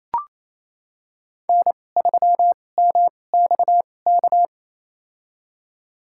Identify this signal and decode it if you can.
Morse code, heard as a keyed tone, about 27 wpm, spaced as for 20 wpm, 700 Hz. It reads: N3MXK
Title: USA Call Signs - 2x Character Spacing - 28wpm